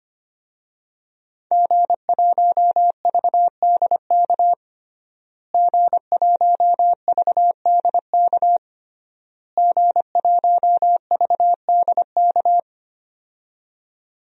Morse code, heard as a keyed tone, about 25 wpm, 700 Hz. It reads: G1VDK G1VDK G1VDK
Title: DXCC-100 Call Signs - 5 Characters - Repeated 3x - 25wpm